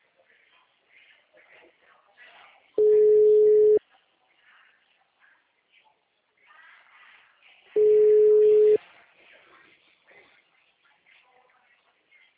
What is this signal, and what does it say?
Ringback tone (Europe, 425 Hz).